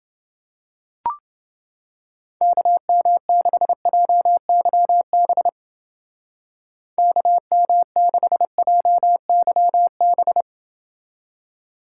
Morse code, heard as a keyed tone, about 28 wpm, 700 Hz. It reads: KM6JYB KM6JYB